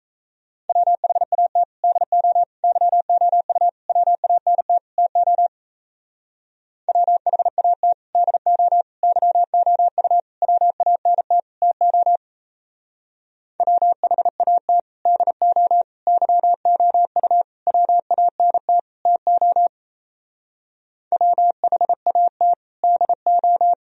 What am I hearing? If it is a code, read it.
Morse code, heard as a keyed tone, about 34 wpm, 700 Hz. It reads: WHAT DO YOU WANT TO WHAT DO YOU WANT TO WHAT DO YOU WANT TO WHAT DO
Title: Speed-Racing - Sets of 5 Words 28wpm